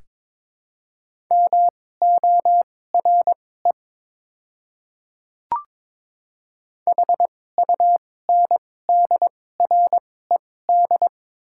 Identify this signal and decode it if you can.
Morse code, heard as a keyed tone, about 22 wpm, spaced as for 16 wpm, 690 Hz. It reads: MORE HUNDRED
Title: 2x Character Spacing - Top 500 Words - 22wpm